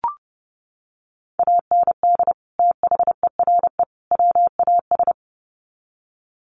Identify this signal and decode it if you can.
Morse code, heard as a keyed tone, about 30 wpm, 700 Hz. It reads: AND THERE WAS
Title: Sets of 3 Words 30wpm